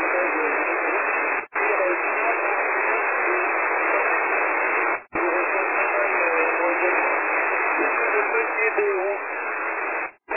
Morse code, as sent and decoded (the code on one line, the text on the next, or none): none